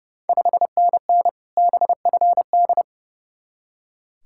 Morse code, heard as a keyed tone, about 31 wpm, 700 Hz. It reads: 5NN BFD
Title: Ontario QSO Party Exchange 30wpm